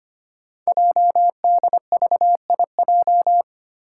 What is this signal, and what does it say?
Morse code, heard as a keyed tone, about 25 wpm, 700 Hz. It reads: JDVIJ
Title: Code Groups - 5 Letter:Numbers - 25wpm